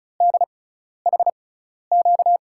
Morse code, heard as a keyed tone, about 36 wpm, spaced as for 12 wpm, 700 Hz. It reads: DHQ
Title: USA Call Signs - 6x Character Spacing - 35wpm